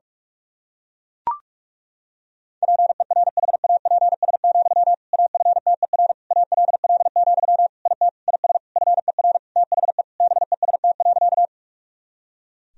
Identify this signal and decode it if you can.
Morse code, heard as a keyed tone, about 42 wpm, 700 Hz. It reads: PERHAPS, AFTER ALL, IT IS FER THE BEST.
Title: Sentences from Top 500 Words and Abbreviations 45wpm